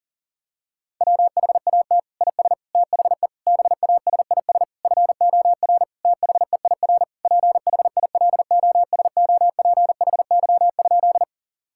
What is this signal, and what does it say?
Morse code, heard as a keyed tone, about 37 wpm, 700 Hz. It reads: WHAT IS THE BASIS FOR THEIR PHILOSOPHY?
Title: Sentences from Top 1000 Words - Review Part 6 - 40wpm